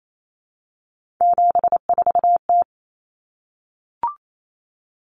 Morse code, heard as a keyed tone, about 28 wpm, 700 Hz. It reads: 74T
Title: Common Cut Numbers - 3-Digit 28wpm